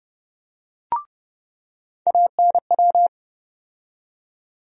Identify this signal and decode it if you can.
Morse code, heard as a keyed tone, about 30 wpm, 700 Hz. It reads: ANW
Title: USA Call Sign Postfixes - 3 Letters 30wpm